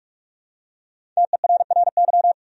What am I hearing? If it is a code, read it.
Morse code, heard as a keyed tone, about 47 wpm, 700 Hz. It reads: TERRY
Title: USA Common Names - 45wpm